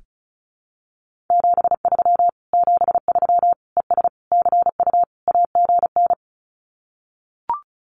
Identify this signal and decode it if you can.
Morse code, heard as a keyed tone, about 34 wpm, 700 Hz. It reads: 73 73 ES CU AGN
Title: Common QSO Phrases Encore 35wpm